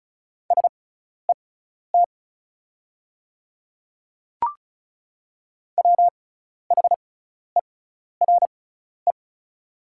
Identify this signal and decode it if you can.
Morse code, heard as a keyed tone, about 36 wpm, spaced as for 12 wpm, 690 Hz. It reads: SET WHERE
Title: 6x Character Spacing - Top 200 Words - 35wpm